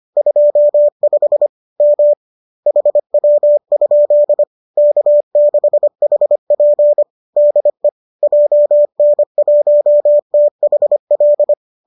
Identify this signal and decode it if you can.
Morse code, heard as a keyed tone, about 25 wpm, 590 Hz. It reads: 25 M HW? K6HP DE JN1THL
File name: qso205 25